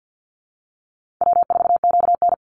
Morse code, heard as a keyed tone, about 50 wpm, 700 Hz. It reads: W4QN
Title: Call Signs - 50wpm